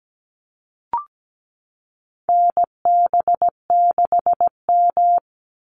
Morse code, heard as a keyed tone, about 17 wpm, 700 Hz. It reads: NB6M
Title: Call Signs Repeated 2x - 17wpm